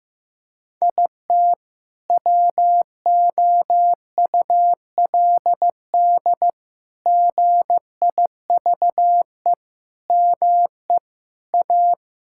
Morse code, heard as a keyed tone, about 15 wpm, 700 Hz. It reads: IT WOULD GIVE ME A
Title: Sentences from Top 1600 Words 15wpm